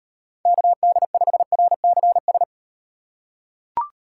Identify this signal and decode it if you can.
Morse code, heard as a keyed tone, about 38 wpm, 700 Hz. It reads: KD5RCS